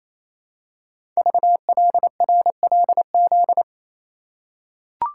Morse code, heard as a keyed tone, about 28 wpm, 700 Hz. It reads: VLRLZ
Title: Code Groups - 5 Letters - 28wpm